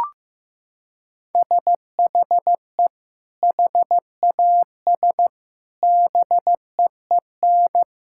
Morse code, heard as a keyed tone, about 15 wpm, 700 Hz. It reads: SHE HAS BEEN
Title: Sentences from Top 1600 Words 15wpm